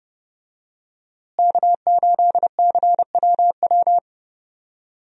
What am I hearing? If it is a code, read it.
Morse code, heard as a keyed tone, about 30 wpm, 700 Hz. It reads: K8CWW
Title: USA Call Signs - 5 Characters - Repeated 2x - 30wpm